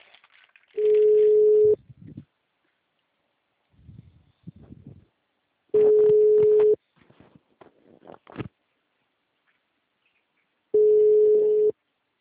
Ringback tone (Europe, 425 Hz).
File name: out- -2016- - - .42898